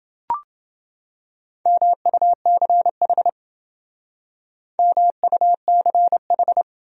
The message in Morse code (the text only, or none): MUCH MUCH